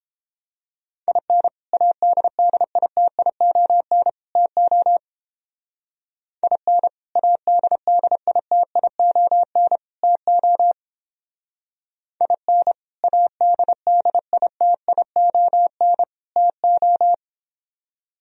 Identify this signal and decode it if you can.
Morse code, heard as a keyed tone, about 29 wpm, 700 Hz. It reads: IN ADDITION TO IN ADDITION TO IN ADDITION TO